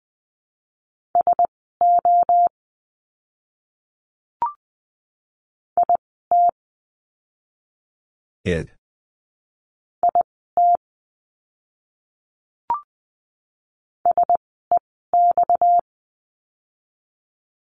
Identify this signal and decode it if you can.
Morse code, heard as a keyed tone, about 20 wpm, spaced as for 15 wpm, 700 Hz. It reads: SO IT IT SEX
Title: 230 Lesson 37f - Letter x - Words 20wpm